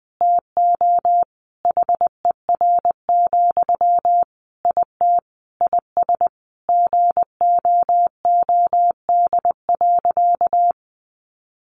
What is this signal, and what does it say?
Morse code, heard as a keyed tone, about 20 wpm, 700 Hz. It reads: TO HER, IT IS GOOD.
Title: Sentences from Top 800 Words 20wpm